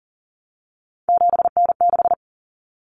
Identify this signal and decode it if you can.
Morse code, heard as a keyed tone, about 40 wpm, 700 Hz. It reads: 7N6
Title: Common Cut Numbers - 3-Digit 40wpm